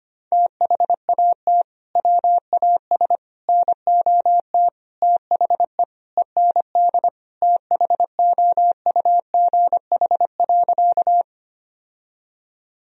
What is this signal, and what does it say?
Morse code, heard as a keyed tone, about 25 wpm, 700 Hz. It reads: THAT WAS NOT THE END THOUGH.
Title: Sentences from Top 1000 Words - Review Part 2 - 25wpm